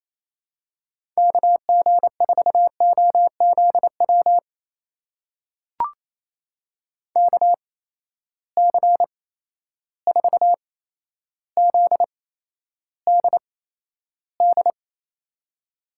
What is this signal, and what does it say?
Morse code, heard as a keyed tone, about 28 wpm, spaced as for 12 wpm, 700 Hz. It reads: KG4OZW KC4ZDD
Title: USA Call Signs - 6 Characters - 8x Character Spacing - 28wpm